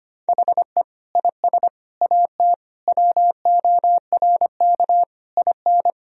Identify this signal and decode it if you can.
Morse code, heard as a keyed tone, about 25 wpm, 700 Hz. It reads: HE IS AT WORK IN
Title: Sentences from Top 1000 Words - Review Part 4 - 25wpm